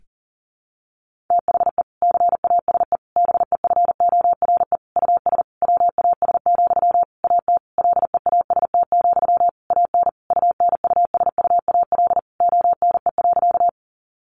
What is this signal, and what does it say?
Morse code, heard as a keyed tone, about 40 wpm, 700 Hz. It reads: THE CASE BEFORE US WAS, AT LEAST, AN UNUSUAL ONE.
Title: Sentences from Top 1400 Words 40wpm